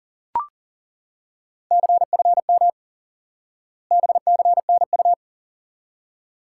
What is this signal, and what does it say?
Morse code, heard as a keyed tone, about 40 wpm, 700 Hz. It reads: CFM BCNU
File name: IQE Course 3 - Lesson 89 - YAGI - Two Elements - 40wpm